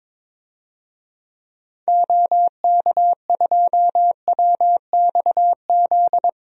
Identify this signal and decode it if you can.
Morse code, heard as a keyed tone, about 22 wpm, 700 Hz. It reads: OK2WXZ
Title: DXCC-100 Call Signs - 6 Characters - 22wpm